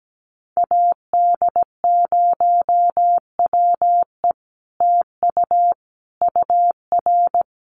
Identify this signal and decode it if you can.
Morse code, heard as a keyed tone, about 17 wpm, 700 Hz. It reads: AD0WE TU UR
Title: Common QSO Phrases Encore 17wpm